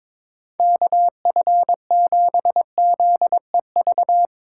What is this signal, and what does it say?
Morse code, heard as a keyed tone, about 22 wpm, 700 Hz. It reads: KF7ZEV
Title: Speed Racing - USA 2x3 Call Signs 22wpm